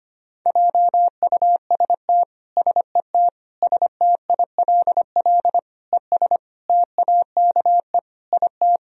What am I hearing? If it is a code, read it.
Morse code, heard as a keyed tone, about 25 wpm, 700 Hz. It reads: JUST SET STILL ES TAKE IT
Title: Sentences from Top 300 Words and Abbreviations 25wpm